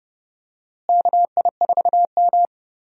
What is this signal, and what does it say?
Morse code, heard as a keyed tone, about 32 wpm, 700 Hz. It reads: KI4M
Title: USA Call Signs - 4 Characters - Repeated 2x - 30wpm